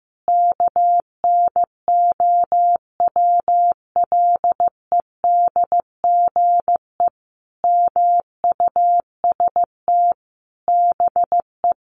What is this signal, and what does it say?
Morse code, heard as a keyed tone, about 15 wpm, 700 Hz. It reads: KNOWLEDGE MUST BE
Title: Sentences from Top 1400 Words 15wpm